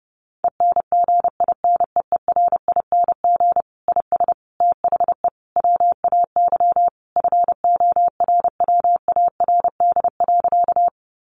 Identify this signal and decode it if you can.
Morse code, heard as a keyed tone, about 28 wpm, 700 Hz. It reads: ENGINEERING IS THE WAY FORWARD.
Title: Sentences from Top 1500 Words 30wpm